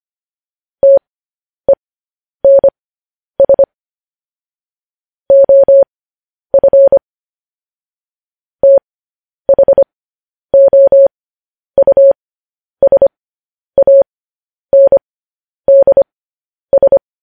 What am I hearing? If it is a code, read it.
Morse code, heard as a keyed tone, about 25 wpm, spaced as for 10 wpm, 550 Hz. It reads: TENS OF THOUSANDS
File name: podcast / News headlines-1x-25x10